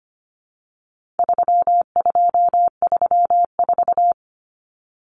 Morse code, heard as a keyed tone, about 25 wpm, 700 Hz. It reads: 3234
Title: Common Cut Numbers - 4-Digit 25wpm